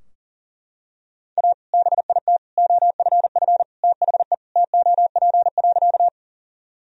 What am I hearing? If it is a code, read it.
Morse code, heard as a keyed tone, about 40 wpm, 700 Hz. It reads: A BIT OFF THE TOP.
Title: Sentences from Top 1000 Words - Review Part 2 - 40wpm